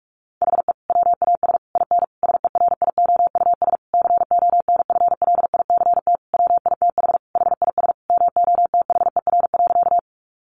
Morse code, heard as a keyed tone, about 44 wpm, 700 Hz. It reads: HE WAS IN SERIOUS CONFLICT WITH HIS MOTHER.
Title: Sentences from Top 1300 Words 45wpm